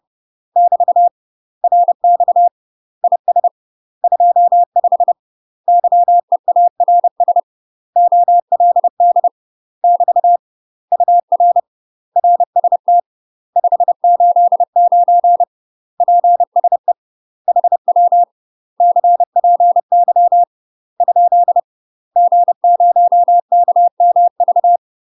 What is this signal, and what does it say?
Morse code, heard as a keyed tone, about 30 wpm, 700 Hz. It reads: <BT> RX IS 25 YEARS OLD <BT> UR RST 589 PSE HW CPY ? G0KMV
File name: UK MASTER QSOs TRAINING/30WPM